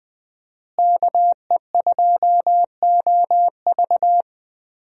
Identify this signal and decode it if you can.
Morse code, heard as a keyed tone, about 21 wpm, 700 Hz. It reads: KE2OV
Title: USA Call Signs - 5 Characters - Repeated 3x - 20wpm